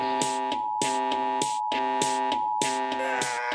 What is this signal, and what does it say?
Emergency Alert System (SAME) tones (the 853 and 960 Hz attention tone), with noise. It is heard from 0.2 to 3.0 s.